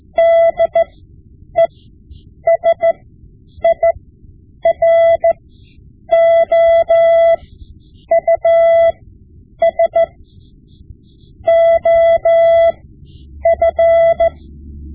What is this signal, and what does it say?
Morse code, heard as a keyed tone, about 14 wpm, spaced as for 8 wpm, 670 Hz. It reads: DESIROUS OF
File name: Morse Lessons / Morse Lessons / Tape-5B